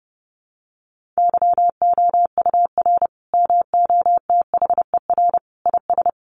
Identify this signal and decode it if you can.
Morse code, heard as a keyed tone, about 30 wpm, 700 Hz. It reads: YOUR MOTHER IS
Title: Sentences from Top 1400 Words 30wpm